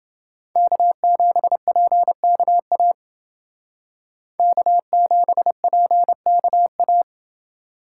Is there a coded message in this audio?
Yes: Morse code from 0.6 to 7.0 s, heard as a keyed tone, about 29 wpm, 700 Hz. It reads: K7PKA K7PKA